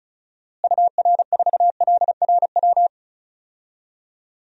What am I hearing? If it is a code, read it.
Morse code, heard as a keyed tone, about 34 wpm, 700 Hz. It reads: UR4LRW
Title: DXCC-100 Call Signs - 6 Characters - Repeated 3x - 35wpm